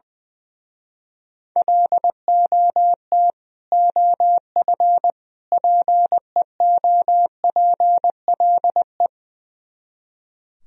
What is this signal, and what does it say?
Morse code, heard as a keyed tone, about 20 wpm, 700 Hz. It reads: LOT OF PEOPLE
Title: Speed-Racing - Sets of 3 Words 20wpm